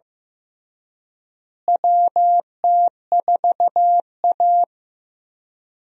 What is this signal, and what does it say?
Morse code, heard as a keyed tone, about 15 wpm, 700 Hz. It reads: WT4A
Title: USA Call Signs - 4 Characters - Repeated 3x - 15wpm